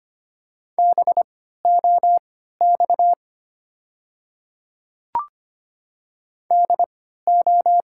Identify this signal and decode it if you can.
Morse code, heard as a keyed tone, about 25 wpm, spaced as for 14 wpm, 690 Hz. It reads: BOX DO
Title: 3x Character Spacing - Top 500 Words - 25wpm